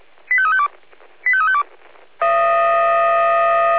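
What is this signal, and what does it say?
Five-tone selective call (ZVEI), heard as quick runs of five tones, tones of about 70 ms. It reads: 85352 85352
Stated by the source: ZVEI five-tone call 85350 received on SDR.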